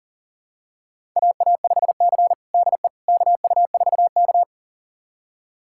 Morse code, heard as a keyed tone, about 40 wpm, 700 Hz. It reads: AA5C DE KU4K
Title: USA Callsign DE Callsign - 4 Characters - 40wpm